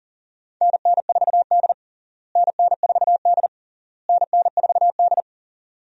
Morse code, heard as a keyed tone, about 40 wpm, 700 Hz. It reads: NN4D NN4D NN4D
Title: USA Call Signs - 4 Characters - Repeated 3x - 40wpm